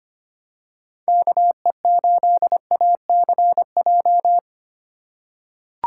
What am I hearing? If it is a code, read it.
Morse code, heard as a keyed tone, about 25 wpm, 700 Hz. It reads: KE8ACJ
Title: USA Call Signs - 6 Characters - 4x Character Spacing - 25wpm